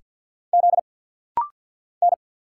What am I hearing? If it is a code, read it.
Morse code, heard as a keyed tone, about 49 wpm, 700 Hz. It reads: Z N
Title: Single Letter-Number Alt - Rapid-Fire 50wpm